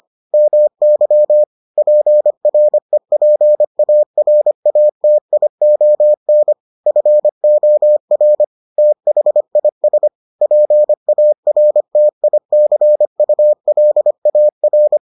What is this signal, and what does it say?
Morse code, heard as a keyed tone, about 25 wpm, 600 Hz. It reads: MY PREPARATION FOR THIS PARTICULAR